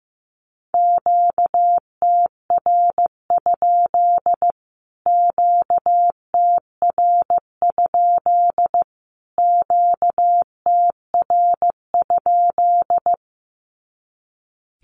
Morse code, heard as a keyed tone, about 15 wpm, 700 Hz. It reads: QTR? QTR? QTR?
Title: Q-Codes Repeated 3x 15wpm